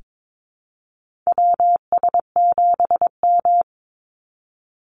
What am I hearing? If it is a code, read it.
Morse code, heard as a keyed tone, about 22 wpm, 700 Hz. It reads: WS7M
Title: Call Signs - 22wpm